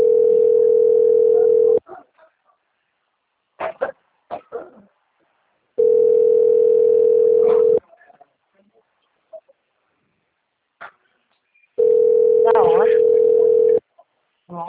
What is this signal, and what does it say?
Ringback tone (North America), 440 + 480 Hz, with speech.
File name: rg-1600-unknown- - - .39234